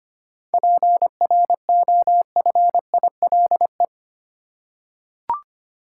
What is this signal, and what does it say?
Morse code, heard as a keyed tone, about 25 wpm, 700 Hz. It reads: PROFILE
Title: Seven Letter Words - 25wpm